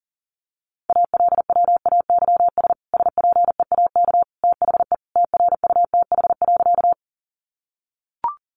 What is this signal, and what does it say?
Morse code, heard as a keyed tone, about 40 wpm, 700 Hz. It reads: ALWAYS SPEAK THE TRUTH.
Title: Sentences from Top 1100 Words 40wpm